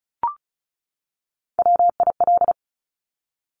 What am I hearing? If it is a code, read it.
Morse code, heard as a keyed tone, about 36 wpm, 700 Hz. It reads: WIL
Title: Top Three Letter Combinations in Words 35wpm